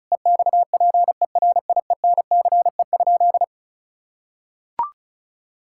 Morse code, heard as a keyed tone, about 34 wpm, 700 Hz. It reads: EXPERIENCE?
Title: Sentences from Top 1600 Words 35wpm